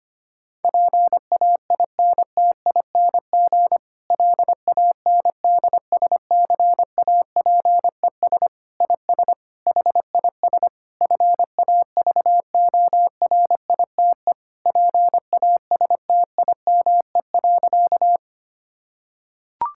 Morse code, heard as a keyed tone, about 25 wpm, 700 Hz. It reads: PAINTING LANDSCAPES IS HIS FAVORITE PASTIME.
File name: practice / 25wpm / Difficult Sentences with Hints - 25wpm